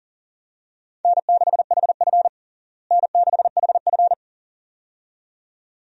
Morse code, heard as a keyed tone, about 40 wpm, 700 Hz. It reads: N6HF N6HF
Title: USA Call Signs - 4 Characters - Repeated 2x - 40wpm